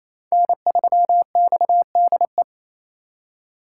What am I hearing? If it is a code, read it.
Morse code, heard as a keyed tone, about 36 wpm, 700 Hz. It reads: N3XDE